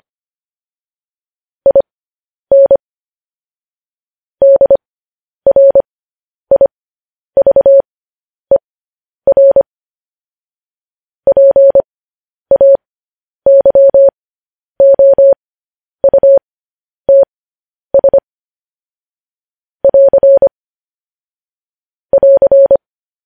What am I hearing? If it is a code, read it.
Morse code, heard as a keyed tone, about 25 wpm, spaced as for 10 wpm, 550 Hz. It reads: IN DRIVER PAYOUTS <AR> <AR>
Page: news-headlines-in-morse-code-at-10-wpm